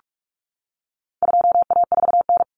Morse code, heard as a keyed tone, about 44 wpm, 700 Hz. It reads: 2A4N
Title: Common Cut Numbers - 4-Digit 45wpm